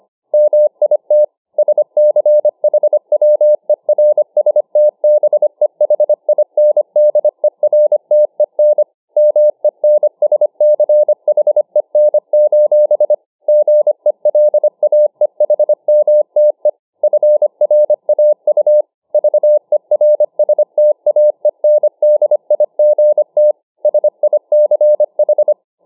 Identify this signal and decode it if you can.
Morse code, heard as a keyed tone, about 25 wpm, 600 Hz. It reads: MIT SCHWERSTBEHINDERTEN MENSCHEN: GELAEHMTE FRAU VERSTAENDIGT SICH